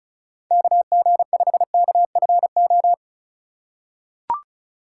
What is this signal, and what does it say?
Morse code, heard as a keyed tone, about 36 wpm, 700 Hz. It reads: KG5KFO
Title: USA Call Signs - 3x Character Spacing - 35wpm